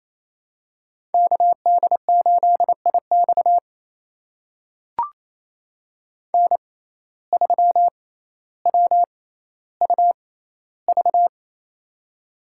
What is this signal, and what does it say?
Morse code, heard as a keyed tone, about 28 wpm, spaced as for 14 wpm, 700 Hz. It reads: KD8IX N3WUV
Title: USA Call Signs - 5 Characters - 6x Character Spacing - 28wpm